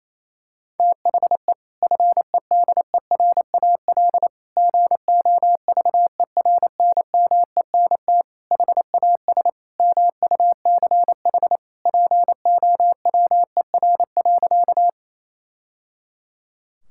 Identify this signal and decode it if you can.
Morse code, heard as a keyed tone, about 28 wpm, 700 Hz. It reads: THE FEDERAL GOVERNMENT HAS MUCH POWER.
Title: Sentences from Top 1000 Words - Review Part 5 - 28wpm